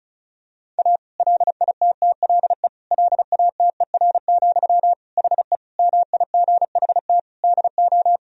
Morse code, heard as a keyed tone, about 36 wpm, 700 Hz. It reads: A LITTLE LATER, HE MIGHT DO
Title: Sentences from Top 1000 Words - Review Part 4 - 35wpm